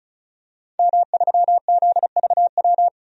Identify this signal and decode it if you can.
Morse code, heard as a keyed tone, about 36 wpm, 700 Hz. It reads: M3ZVW
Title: DXCC-100 Call Signs - 5 Characters - 35wpm